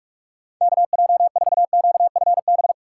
Morse code, heard as a keyed tone, about 46 wpm, 700 Hz. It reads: KJ4QFB
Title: USA Call Signs - 6 Characters - 2x Character Spacing - 45wpm